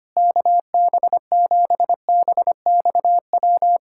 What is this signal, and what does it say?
Morse code, heard as a keyed tone, about 25 wpm, 700 Hz. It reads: KB7BXW
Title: USA Callsign DE Callsign 25wpm